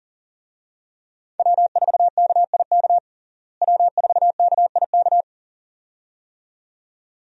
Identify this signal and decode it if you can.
Morse code, heard as a keyed tone, about 40 wpm, 700 Hz. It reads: W4KIK W4KIK
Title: USA Call Signs - 5 Characters - Repeated 2x - 40wpm